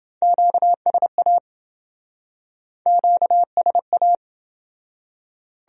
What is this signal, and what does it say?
Morse code, heard as a keyed tone, about 27 wpm, 700 Hz. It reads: QSA QSA